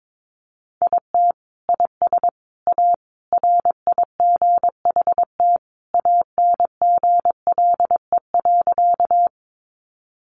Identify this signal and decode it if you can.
Morse code, heard as a keyed tone, about 22 wpm, 700 Hz. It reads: IT IS A RIGHT ANGLE.
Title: Sentences from Top 1500 Words 22wpm